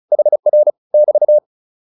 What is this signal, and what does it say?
Morse code, heard as a keyed tone, about 34 wpm, 600 Hz. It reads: HR <BT>
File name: QSO204 35